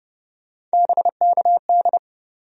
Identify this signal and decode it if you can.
Morse code, heard as a keyed tone, about 30 wpm, 700 Hz. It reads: BKD
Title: USA Call Sign Postfixes - 3 Letters 30wpm